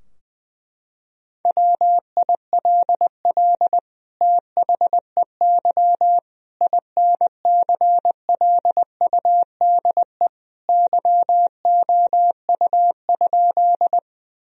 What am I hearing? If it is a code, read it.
Morse code, heard as a keyed tone, about 21 wpm, 700 Hz. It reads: WILL THEY INCLUDE YOU?
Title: Sentences from Top 1000 Words - Review Part 3 - 20wpm